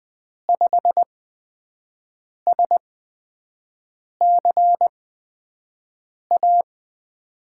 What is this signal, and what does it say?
Morse code, heard as a keyed tone, about 20 wpm, spaced as for 5 wpm, 700 Hz. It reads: 5SCA
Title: USA Call Signs - 6 Characters - 8x Character Spacing - 20wpm